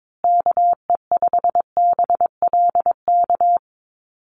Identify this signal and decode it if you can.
Morse code, heard as a keyed tone, about 22 wpm, 700 Hz. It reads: KE5BLK
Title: USA Call Signs Repeated 2x 22wpm